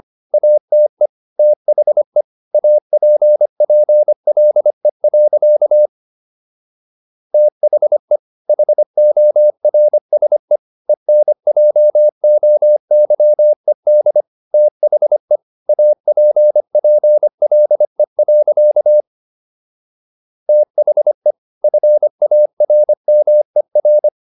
Morse code, heard as a keyed tone, about 25 wpm, 600 Hz. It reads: ATE THE APPLE. THE HORSE ENJOYED THE APPLE. THE FARMER